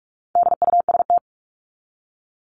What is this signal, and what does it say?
Morse code, heard as a keyed tone, about 44 wpm, 700 Hz. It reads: DUST
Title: Top 1751-2000 Words 45wpm